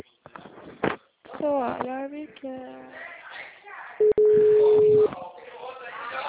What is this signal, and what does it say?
Ringback tone (Europe, 425 Hz), with speech.